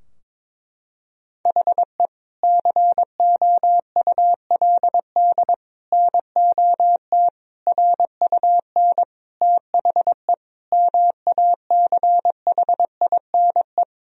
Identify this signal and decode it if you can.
Morse code, heard as a keyed tone, about 22 wpm, 700 Hz. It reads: HE COULD NOT RUN THE MACHINE
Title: Sentences from Top 1000 Words - Review Part 2 - 22wpm